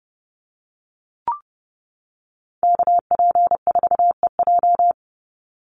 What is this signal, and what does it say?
Morse code, heard as a keyed tone, about 30 wpm, 700 Hz. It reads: KP4EJ
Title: Call Signs - 30wpm